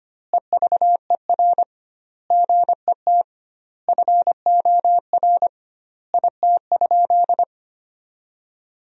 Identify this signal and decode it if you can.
Morse code, heard as a keyed tone, about 25 wpm, 700 Hz. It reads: EVER GET FOR IT?
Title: 2x Word-Spacing - Sentences from Top 400 Words - 25wpm